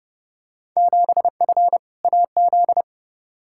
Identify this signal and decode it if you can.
Morse code, heard as a keyed tone, about 30 wpm, 700 Hz. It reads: 7F AZ
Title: Field Day Exchanges Encore - 30wpm